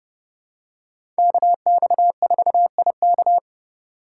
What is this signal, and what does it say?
Morse code, heard as a keyed tone, about 30 wpm, 700 Hz. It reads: KX4IK
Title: USA Call Signs - 5 Characters - 4x Character Spacing - 30wpm